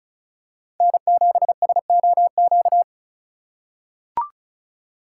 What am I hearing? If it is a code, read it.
Morse code, heard as a keyed tone, about 36 wpm, 700 Hz. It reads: N7SOQ
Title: USA Call Signs - 5 Characters - 2x Character Spacing - 35wpm